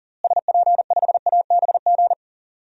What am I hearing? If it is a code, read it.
Morse code, heard as a keyed tone, about 40 wpm, 700 Hz. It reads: SP5ABG
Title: DXCC-100 Call Signs - 6 Characters - Repeated 2x - 40wpm